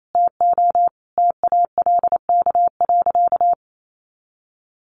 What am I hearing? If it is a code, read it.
Morse code, heard as a keyed tone, about 28 wpm, 700 Hz. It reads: TO TALK.
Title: Sentences from Top 400 Words 28wpm